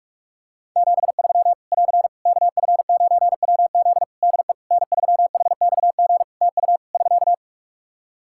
Morse code, heard as a keyed tone, about 44 wpm, 700 Hz. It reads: 73 <AR> KF9WZ DE N3HXG TU <SK>
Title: Practice QSO - Exchange 4 - Ending - 45wpm